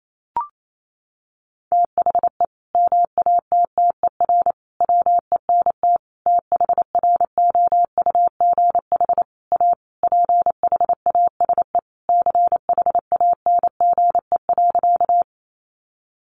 Morse code, heard as a keyed tone, about 28 wpm, 700 Hz. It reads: THE MATTER WENT THROUGH A PHASE CHANGE.
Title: Sentences from Top 1200 Words 28wpm